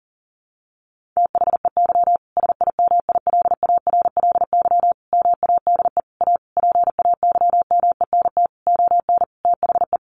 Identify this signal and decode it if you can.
Morse code, heard as a keyed tone, about 40 wpm, 700 Hz. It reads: THEY SIMILARLY MADE A PAYMENT ON THE
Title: Sentences from Top 1400 Words 40wpm